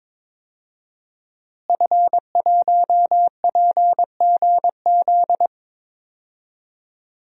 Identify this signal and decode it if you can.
Morse code, heard as a keyed tone, about 22 wpm, 700 Hz. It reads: F1PGZ
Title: DXCC-100 Call Signs - 5 Characters - Repeated 2x - 22wpm